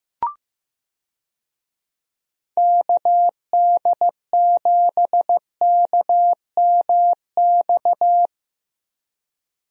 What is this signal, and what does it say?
Morse code, heard as a keyed tone, about 15 wpm, 700 Hz. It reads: KD7KMX
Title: USA Call Signs - 6 Characters - Repeated 2x - 15wpm